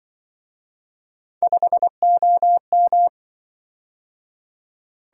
Morse code, heard as a keyed tone, about 26 wpm, 700 Hz. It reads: 5OM